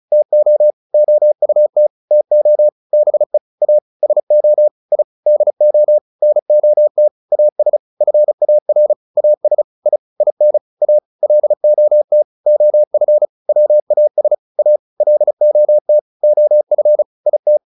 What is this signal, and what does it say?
Morse code, heard as a keyed tone, about 34 wpm, 600 Hz. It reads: TO OUT TO BE A SO I DO NOT AS FAR AS I IN A LOT OF WAS A LOT OF IT